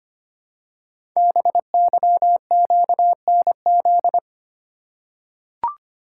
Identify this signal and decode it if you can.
Morse code, heard as a keyed tone, about 25 wpm, 700 Hz. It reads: BYQNZ
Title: Code Groups - 5 Letters - 25wpm